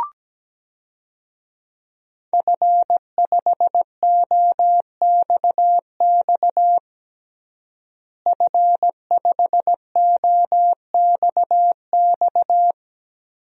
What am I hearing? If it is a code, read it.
Morse code, heard as a keyed tone, about 17 wpm, 700 Hz. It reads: F5OXX F5OXX
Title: DXCC-100 Call Signs - 5 Characters - Repeated 2x - 17wpm